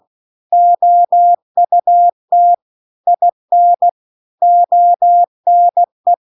Morse code, heard as a keyed tone, about 16 wpm, 700 Hz. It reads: OUT IN ONE
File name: CW-The-Big-Twoheaded-River-2-16WPM